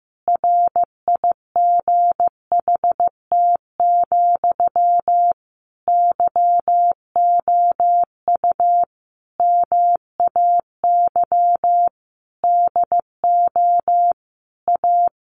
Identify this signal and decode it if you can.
Morse code, heard as a keyed tone, about 15 wpm, 700 Hz. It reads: RIGHT, YOU MAY DO A
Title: Sentences from Top 900 Words 15wpm